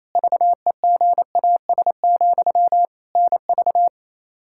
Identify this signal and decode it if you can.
Morse code, heard as a keyed tone, about 28 wpm, 700 Hz. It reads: VEGAS, NV
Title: Practice QSO - Exchange 1 - Beginning - 28wpm